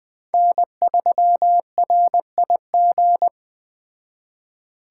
Morse code, heard as a keyed tone, about 20 wpm, 700 Hz. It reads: N3RIG